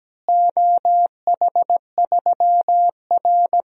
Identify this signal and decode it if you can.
Morse code, heard as a keyed tone, about 17 wpm, 700 Hz. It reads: OH3R
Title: DXCC-100 Call Signs - 4 Characters - Repeated 2x - 17wpm